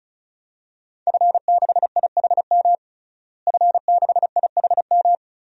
Morse code, heard as a keyed tone, about 36 wpm, 700 Hz. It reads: F6IHM F6IHM
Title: DXCC-100 Call Signs - 5 Characters - Repeated 3x - 35wpm